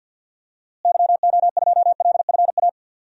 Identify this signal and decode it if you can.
Morse code, heard as a keyed tone, about 51 wpm, 700 Hz. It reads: YO2LFA
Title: DXCC-100 Call Signs - 6 Characters - Repeated 2x - 50wpm